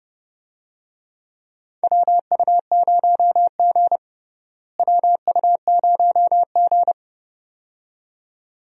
Morse code, heard as a keyed tone, about 30 wpm, 700 Hz. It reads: WU0G WU0G
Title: USA Call Signs - 4 Characters - Repeated 2x - 30wpm